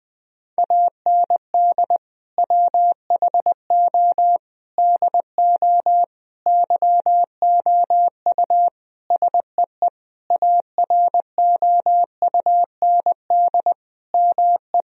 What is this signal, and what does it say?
Morse code, heard as a keyed tone, about 20 wpm, 700 Hz. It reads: AND WHO DO YOU SEE AROUND ME
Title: Sentences from Top 1000 Words - Review Part 3 - 20wpm